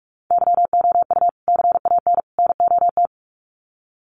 Morse code, heard as a keyed tone, about 44 wpm, 700 Hz. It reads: YOU CAN NOT
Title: Sets of 3 Words 45wpm